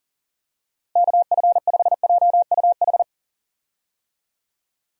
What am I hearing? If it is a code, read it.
Morse code, heard as a keyed tone, about 37 wpm, 700 Hz. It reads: KF5JUH
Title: USA Call Signs - 6 Characters - Repeated 2x - 40wpm